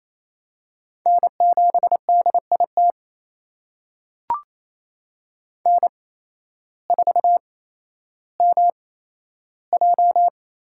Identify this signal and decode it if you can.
Morse code, heard as a keyed tone, about 27 wpm, spaced as for 13 wpm, 700 Hz. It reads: N7DIT N4MJ
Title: USA Call Signs - 8x Character Spacing - 28wpm